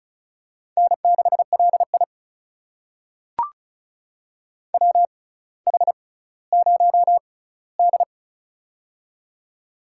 Morse code, heard as a keyed tone, about 36 wpm, spaced as for 18 wpm, 700 Hz. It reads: N6LI WH0D
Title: USA Call Signs - 6x Character Spacing - 35wpm